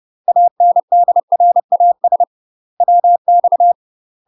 Morse code, heard as a keyed tone, about 30 wpm, 700 Hz. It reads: ANDRAS WX
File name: UK MASTER QSOs TRAINING / 30WPM / UK MASTER QSOs 0009